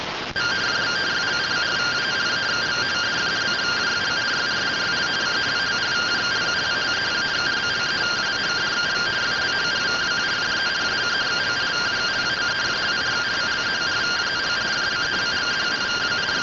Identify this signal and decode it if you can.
Two-tone teleprinter signal, about 170 Hz shift.